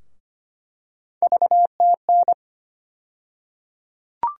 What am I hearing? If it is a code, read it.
Morse code, heard as a keyed tone, about 25 wpm, 700 Hz. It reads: VTN
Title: USA Call Sign Postfixes - 3 Letters 25wpm